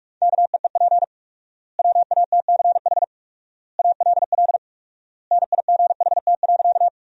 Morse code, heard as a keyed tone, about 46 wpm, 700 Hz. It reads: KEEP WATCH ALL NIGHT.
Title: 4x Word-Spacing - Sentences from Top 300 Words - 45wpm